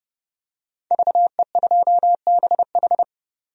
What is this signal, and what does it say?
Morse code, heard as a keyed tone, about 31 wpm, 700 Hz. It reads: VE2BH
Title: DXCC-100 Call Signs - 5 Characters - 30wpm